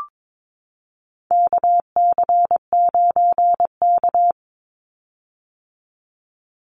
Morse code, heard as a keyed tone, about 22 wpm, 700 Hz. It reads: KC9K
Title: Call Signs - 22wpm